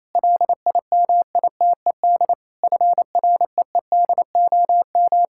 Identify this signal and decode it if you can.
Morse code, heard as a keyed tone, about 28 wpm, 700 Hz. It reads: LIMITED FREEDOM
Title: Sentences from Top 1000 Words - Review Part 6 - 28wpm